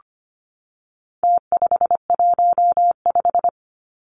Morse code, heard as a keyed tone, about 25 wpm, 700 Hz. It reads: T515
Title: Numbers - Contest Style - 4-Digit 25wpm